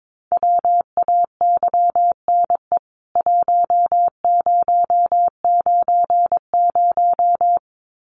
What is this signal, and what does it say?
Morse code, heard as a keyed tone, about 22 wpm, 700 Hz. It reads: WAYNE 1090
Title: CWT Exchange 22wpm